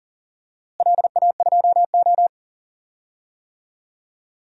Morse code, heard as a keyed tone, about 40 wpm, 700 Hz. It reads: LA2O
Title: DXCC-100 Call Signs - 4 Characters - Repeated 3x - 40wpm